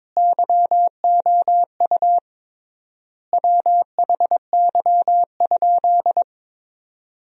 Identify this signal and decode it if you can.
Morse code, heard as a keyed tone, about 22 wpm, 700 Hz. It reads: YOU WHY?
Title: 3x Word-Spacing - Sentences from Top 200 Words - 22wpm